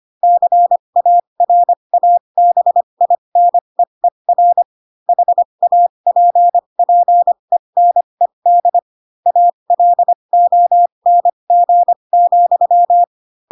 Morse code, heard as a keyed tone, about 25 wpm, spaced as for 22 wpm, 700 Hz. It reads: CARABINEER HAPPENED ALONG,
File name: Pinocchio / Pinocchio ch3 18 25 ik1hge